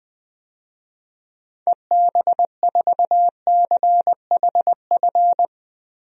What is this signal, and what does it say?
Morse code, heard as a keyed tone, about 20 wpm, 700 Hz. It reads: EB4CHF
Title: DXCC-100 Call Signs - 6 Characters - 20wpm